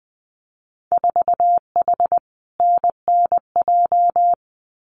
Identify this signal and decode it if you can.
Morse code, heard as a keyed tone, about 20 wpm, 700 Hz. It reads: 4H NNJ
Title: Winter Field Day Exchanges Encore 20wpm